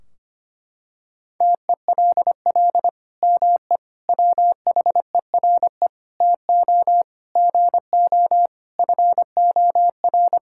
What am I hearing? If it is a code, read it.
Morse code, heard as a keyed tone, about 25 wpm, 700 Hz. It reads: TELL ME WHERE TO GO FOR
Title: Sentences from Top 1600 Words 25wpm